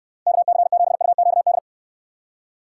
Morse code, heard as a keyed tone, about 62 wpm, 700 Hz. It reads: KC7UQD